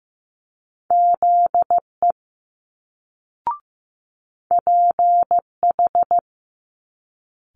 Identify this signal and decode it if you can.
Morse code, heard as a keyed tone, about 15 wpm, 700 Hz. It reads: ZE PH
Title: Multi-letter Phonemes Quick Fire 15wpm